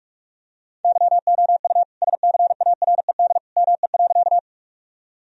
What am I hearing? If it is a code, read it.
Morse code, heard as a keyed tone, about 41 wpm, 700 Hz. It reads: YOU SCARED ME.
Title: Sentences from Top 1000 Words - Review Part 4 - 45wpm